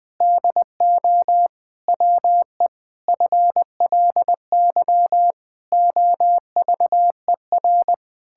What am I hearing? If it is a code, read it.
Morse code, heard as a keyed tone, about 20 wpm, 700 Hz. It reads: DO WE FLY OVER
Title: Pangram Challenge v2 - 20wpm